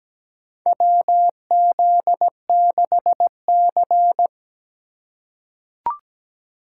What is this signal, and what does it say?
Morse code, heard as a keyed tone, about 17 wpm, 700 Hz. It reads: WZ6C
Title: USA Call Signs - 4 Characters - 2x Character Spacing - 17wpm